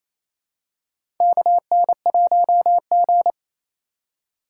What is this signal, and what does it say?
Morse code, heard as a keyed tone, about 27 wpm, 700 Hz. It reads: KN1G
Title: USA Call Signs - 4x Character Spacing - 28wpm